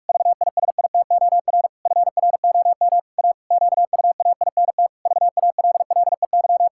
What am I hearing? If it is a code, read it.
Morse code, heard as a keyed tone, about 46 wpm, 700 Hz. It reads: VISITOR FROM A QUAINT VALLEY
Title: Pangram Challenge v2 - 45wpm